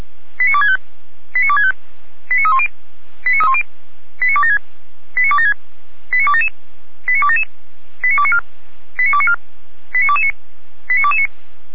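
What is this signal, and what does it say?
Five-tone selective call (ZVEI), heard as quick runs of five tones, tones of about 70 ms. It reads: 89265 89265 89310 89276 89276 89277 89277 89294 89294 89229 89229
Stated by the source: ZVEI five-tone selective call 89337, received by SDR and archived by the Feuerwehr Orlamünde fire brigade (f85360).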